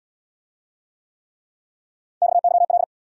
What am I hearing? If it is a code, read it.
Morse code, heard as a keyed tone, about 77 wpm, 700 Hz. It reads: XYL